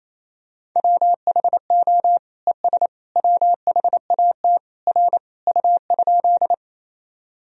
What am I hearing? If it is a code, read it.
Morse code, heard as a keyed tone, about 28 wpm, 700 Hz. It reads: WHO ES WHAT R U?
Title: Sentences from Top 100 Words and Abbreviations 28wpm